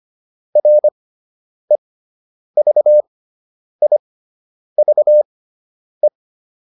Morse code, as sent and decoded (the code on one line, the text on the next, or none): .-. ...- .. ...- .
REVIVE